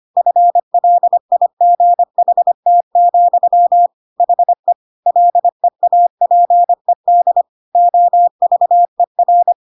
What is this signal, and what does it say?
Morse code, heard as a keyed tone, about 25 wpm, 700 Hz. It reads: FLIGHT, HE LEAPED OVER
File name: Pinocchio / Pinocchio ch4 25 25 ik1hge